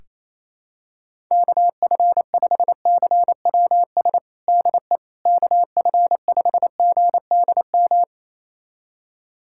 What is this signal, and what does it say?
Morse code, heard as a keyed tone, about 27 wpm, 700 Hz. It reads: KF5CWS DE KF5GDM
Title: USA Callsign DE Callsign - 6 Characters - 28wpm